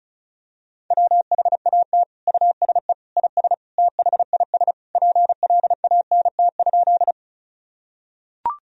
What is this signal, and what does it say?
Morse code, heard as a keyed tone, about 36 wpm, 700 Hz. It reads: WHAT USE IS THIS PLANT?
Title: Sentences from Top 1000 Words - Review Part 1 - 35wpm